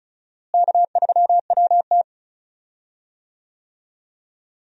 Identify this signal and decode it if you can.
Morse code, heard as a keyed tone, about 34 wpm, 700 Hz. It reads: K3WT
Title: USA Call Signs - 4 Characters - Repeated 3x - 35wpm